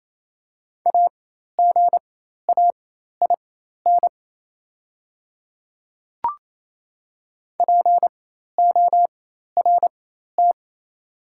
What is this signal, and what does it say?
Morse code, heard as a keyed tone, about 28 wpm, spaced as for 13 wpm, 700 Hz. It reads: AGAIN PORT
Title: 4x Character Spacing - Top 300 Words - 28wpm